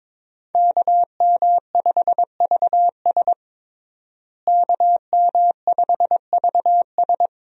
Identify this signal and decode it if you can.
Morse code, heard as a keyed tone, about 22 wpm, 700 Hz. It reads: KM5VS KM5VS
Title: USA Call Signs - 5 Characters - Repeated 3x - 22wpm